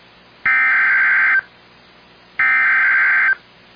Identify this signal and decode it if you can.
Emergency Alert System (SAME) tones (header bursts), heard as harsh buzzing bursts. It is heard from 0.5 to 3.3 s.